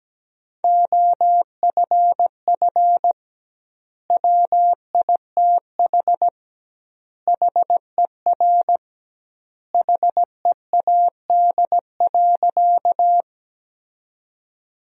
Morse code, heard as a keyed tone, about 17 wpm, 700 Hz. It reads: OFF WITH HER HEAD.
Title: 2x Word-Spacing - Sentences from Top 300 Words - 17wpm